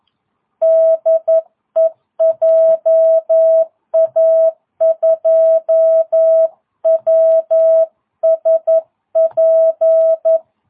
Morse code, heard as a keyed tone, about 11 wpm, 660 Hz. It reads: DE JA2WSP
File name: jh8cbh / 2022cw